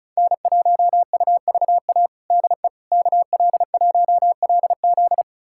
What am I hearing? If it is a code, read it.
Morse code, heard as a keyed tone, about 36 wpm, 700 Hz. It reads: N1UVA DE KL1LZ
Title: USA Callsign DE Callsign - 5 Characters - 35wpm